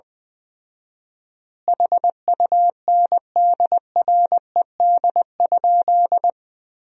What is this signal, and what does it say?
Morse code, heard as a keyed tone, about 20 wpm, 700 Hz. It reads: HUNDRED?
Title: 4x Word-Spacing - Sentences from Top 400 Words - 20wpm